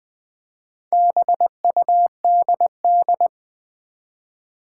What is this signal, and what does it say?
Morse code, heard as a keyed tone, about 20 wpm, 700 Hz. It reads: BUDD
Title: Build A Word - 1 Letter At A Time - 5-Letter Words - 20wpm